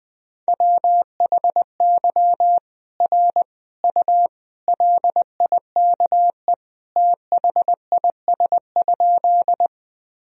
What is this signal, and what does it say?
Morse code, heard as a keyed tone, about 20 wpm, 700 Hz. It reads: WHY R U LIKE THIS?
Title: Sentences from Top 200 Words and Abbreviations 20wpm